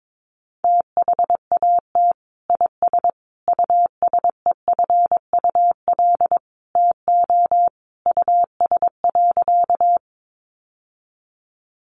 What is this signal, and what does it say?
Morse code, heard as a keyed tone, about 22 wpm, 700 Hz. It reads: THAT IS USEFUL TO US.
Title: Sentences from Top 700 Words 22wpm